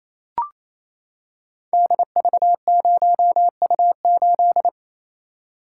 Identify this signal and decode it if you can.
Morse code, heard as a keyed tone, about 27 wpm, 700 Hz. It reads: DV0U8
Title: Code Groups - 5 Letter:Numbers - 28wpm